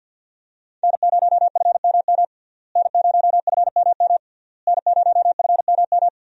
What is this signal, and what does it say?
Morse code, heard as a keyed tone, about 48 wpm, 700 Hz. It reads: N0FMM N0FMM N0FMM
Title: USA Call Signs - 5 Characters - Repeated 3x - 50wpm